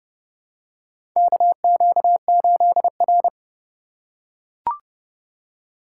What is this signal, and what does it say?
Morse code, heard as a keyed tone, about 30 wpm, 700 Hz. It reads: KQ8R
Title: USA Call Signs - 4 Characters - 3x Character Spacing - 30wpm